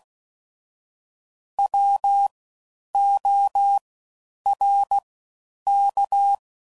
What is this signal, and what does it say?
Morse code, heard as a keyed tone, about 16 wpm, 800 Hz. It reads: W O R K
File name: Suffixes / Special INGs / Fast Ings